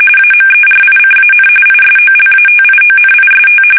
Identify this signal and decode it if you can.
RTTY, about 45.45 baud. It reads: AMERICAN ISOLATIONIST
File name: fsk850